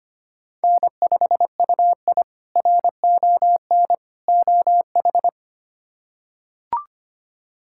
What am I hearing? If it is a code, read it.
Morse code, heard as a keyed tone, about 25 wpm, 700 Hz. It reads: N5UI RON OH
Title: K1USN SST QSO Practice - 25wpm